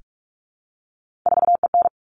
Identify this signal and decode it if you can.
Morse code, heard as a keyed tone, about 44 wpm, 700 Hz. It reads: 4EN
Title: All Cut Numbers - 3-Digit 45wpm